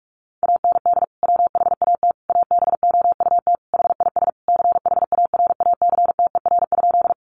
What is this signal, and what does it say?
Morse code, heard as a keyed tone, about 44 wpm, 700 Hz. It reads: AND WHAT ABOUT HIS CHARACTER?
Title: Sentences from Top 1200 Words 45wpm